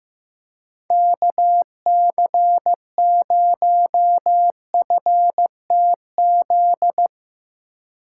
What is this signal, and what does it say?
Morse code, heard as a keyed tone, about 15 wpm, 700 Hz. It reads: KC0FTZ
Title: USA Call Signs - 6 Characters - 3x Character Spacing - 15wpm